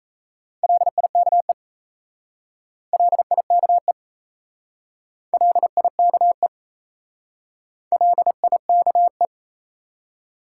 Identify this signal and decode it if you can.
Morse code, heard as a keyed tone, about 37 wpm, 700 Hz. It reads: LIKE LIKE LIKE LIKE